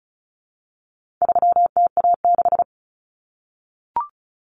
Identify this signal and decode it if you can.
Morse code, heard as a keyed tone, about 34 wpm, 700 Hz. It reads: 3TA6
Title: Common Cut Numbers - 4-Digit 35wpm